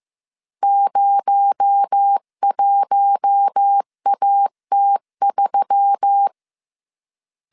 Morse code, heard as a keyed tone, about 15 wpm, 800 Hz. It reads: 01AT3
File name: CW / DET / 1 6 1500C S C 14wpm 2951